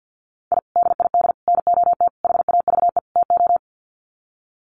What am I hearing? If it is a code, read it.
Morse code, heard as a keyed tone, about 52 wpm, 700 Hz. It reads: I DID NOT HAVE TO
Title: Sets of 5 Words 50wpm